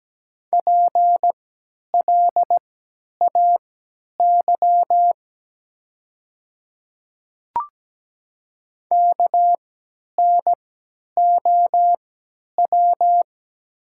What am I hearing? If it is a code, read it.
Morse code, heard as a keyed tone, about 17 wpm, spaced as for 10 wpm, 700 Hz. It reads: PLAY KNOW
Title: 3x Character Spacing - Top 200 Words - 17wpm